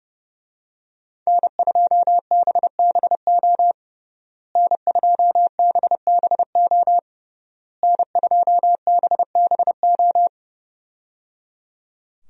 Morse code, heard as a keyed tone, about 30 wpm, 700 Hz. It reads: N2BBO N2BBO N2BBO
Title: USA Call Signs - 5 Characters - Repeated 3x - 30wpm